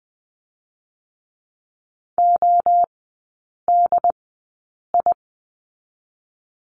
Morse code, heard as a keyed tone, about 20 wpm, spaced as for 8 wpm, 700 Hz. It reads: ODI
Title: 067 Lesson 12e - Letter d - Three Characters 20wpm